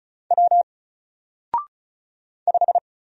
Morse code, heard as a keyed tone, about 33 wpm, 700 Hz. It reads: W 5
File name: Single-Letter-Number-35wpm